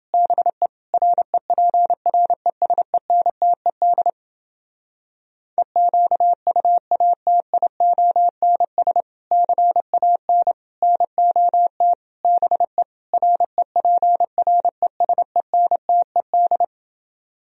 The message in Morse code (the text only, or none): BE REPRESENTED EQUATIONS CAN NOT BE REPRESENTED